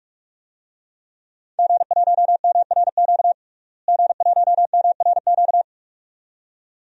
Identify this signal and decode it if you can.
Morse code, heard as a keyed tone, about 42 wpm, 700 Hz. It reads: G1MRQ G1MRQ
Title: DXCC-100 Call Signs - 5 Characters - Repeated 2x - 45wpm